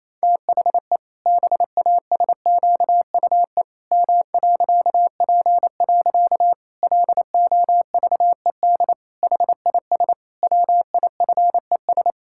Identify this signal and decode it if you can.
Morse code, heard as a keyed tone, about 28 wpm, 700 Hz. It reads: THE BASQUE M.P. LOVED HIS WIFES
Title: Pangram Challenge v2 - 28wpm